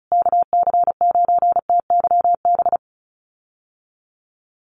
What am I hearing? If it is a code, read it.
Morse code, heard as a keyed tone, about 34 wpm, 700 Hz. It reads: KC9TYB
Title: USA Call Signs Repeated 3x 35wpm